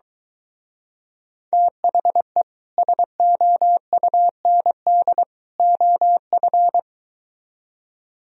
Morse code, heard as a keyed tone, about 23 wpm, 700 Hz. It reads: THE SOUND OF